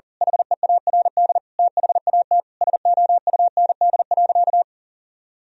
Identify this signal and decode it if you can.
Morse code, heard as a keyed tone, about 40 wpm, 700 Hz. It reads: HEARD THAT SOUND.
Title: Sentences from Top 1000 Words - Review Part 1 - 40wpm